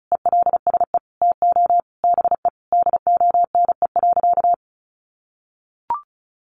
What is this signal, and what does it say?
Morse code, heard as a keyed tone, about 34 wpm, 700 Hz. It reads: ELSE TO BE DONE.
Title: Sentences from Top 800 Words 35wpm